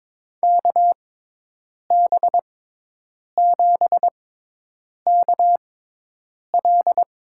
Morse code, heard as a keyed tone, about 22 wpm, spaced as for 8 wpm, 700 Hz. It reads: KB7KL
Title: USA Call Signs - 6 Characters - 6x Character Spacing - 22wpm